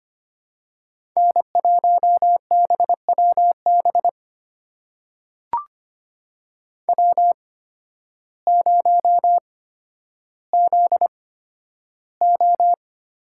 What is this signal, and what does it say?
Morse code, heard as a keyed tone, about 25 wpm, spaced as for 12 wpm, 700 Hz. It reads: N1BWB W0ZO
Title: USA Call Signs - 5 Characters - 8x Character Spacing - 25wpm